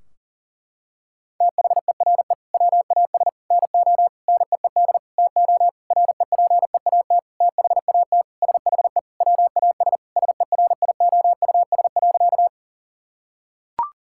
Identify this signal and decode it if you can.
Morse code, heard as a keyed tone, about 40 wpm, 700 Hz. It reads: THERE WAS NO NEED TO REPEAT THAT SHE WAS SERIOUS.
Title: Sentences from Top 1600 Words 40wpm